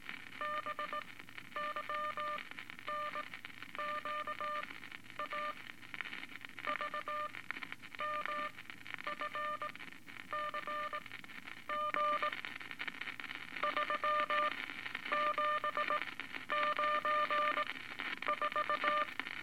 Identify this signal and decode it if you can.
Morse code, heard as a keyed tone, about 18 wpm, spaced as for 10 wpm, 1200 Hz, with noise. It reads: BYNQA VMFCG 3794
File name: Eight Words Per Minute